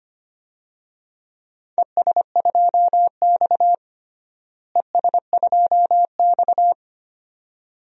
Morse code, heard as a keyed tone, about 25 wpm, 700 Hz. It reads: ES2X ES2X
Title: DXCC-100 Call Signs - 4 Characters - Repeated 2x - 25wpm